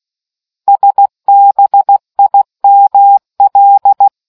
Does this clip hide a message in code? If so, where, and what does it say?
Yes: Morse code from 0.7 to 4.1 s, heard as a keyed tone, about 16 wpm, 800 Hz. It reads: SBIML